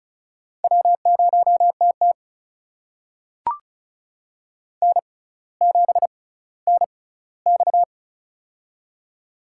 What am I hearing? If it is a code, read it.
Morse code, heard as a keyed tone, about 36 wpm, spaced as for 18 wpm, 700 Hz. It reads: W0TT N7NX
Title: USA Call Signs - 4 Characters - 6x Character Spacing - 35wpm